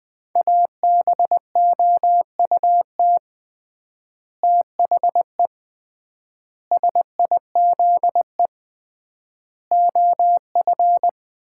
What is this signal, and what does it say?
Morse code, heard as a keyed tone, about 19 wpm, 700 Hz. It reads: ABOUT THE SIZE OF
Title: 3x Word-Spacing - Sets of 5 Words - 20wpm